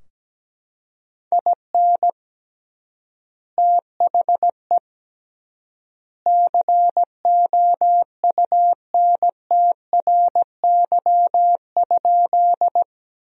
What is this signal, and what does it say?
Morse code, heard as a keyed tone, about 17 wpm, 700 Hz. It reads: IN THE COUNTRY?
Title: 3x Word-Spacing - Sentences from Top 300 Words - 17wpm